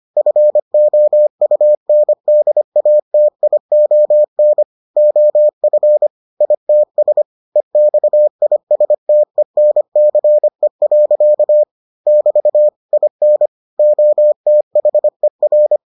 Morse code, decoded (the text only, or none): FOUNDATION OF ITS EXISTENCE. <BT> IN OTHER